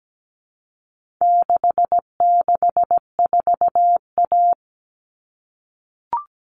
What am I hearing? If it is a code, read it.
Morse code, heard as a keyed tone, about 17 wpm, 700 Hz. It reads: 664A
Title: All Cut Numbers - 4-Digit 17wpm